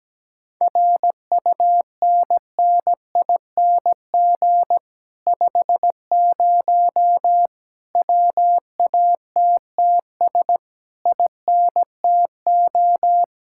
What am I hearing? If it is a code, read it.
Morse code, heard as a keyed tone, about 17 wpm, 700 Hz. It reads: RUNNING 50 WATTS INTO
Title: Speed-Racing Common QSO Phrases Encore - 17wpm